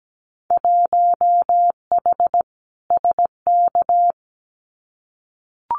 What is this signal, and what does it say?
Morse code, heard as a keyed tone, about 17 wpm, 700 Hz. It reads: 1H SK
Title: Winter Field Day Exchanges 17wpm